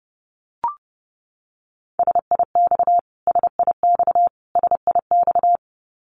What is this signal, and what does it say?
Morse code, heard as a keyed tone, about 30 wpm, 700 Hz. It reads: SIX SIX SIX
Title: Top 500 Words Repeated 30wpm